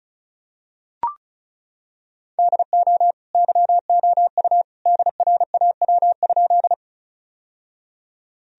Morse code, heard as a keyed tone, about 36 wpm, 700 Hz. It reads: DO YOU DRAW?
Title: Sentences from Top 1000 Words - Review Part 1 - 35wpm